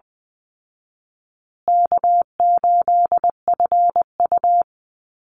Morse code, heard as a keyed tone, about 20 wpm, 700 Hz. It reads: K8FU
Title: Easy Call Signs Repeated 3x - 20wpm